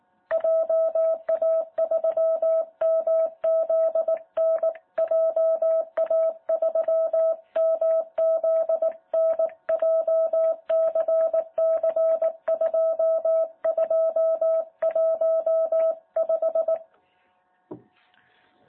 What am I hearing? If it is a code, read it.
Morse code, heard as a keyed tone, about 18 wpm, 650 Hz. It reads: JA3MZN JA3MZN JCC2215
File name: jh8cbh / 2021cw